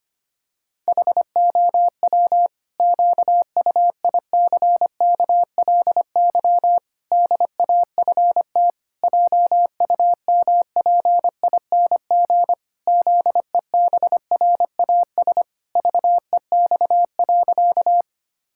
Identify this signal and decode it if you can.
Morse code, heard as a keyed tone, about 25 wpm, 700 Hz. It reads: HOW QUICKLY DAFT JUMPING ZEBRAS VEX.
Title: Pangram Challenge v2 - 25wpm